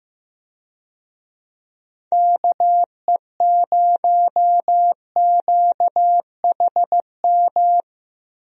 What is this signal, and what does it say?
Morse code, heard as a keyed tone, about 15 wpm, 700 Hz. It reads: KE0QHM
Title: USA Call Signs - 6 Characters - Repeated 2x - 15wpm